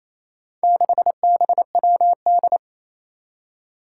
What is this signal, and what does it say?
Morse code, heard as a keyed tone, about 29 wpm, 700 Hz. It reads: 6BWD
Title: Four Challenging Letter-Number Combinations 28wpm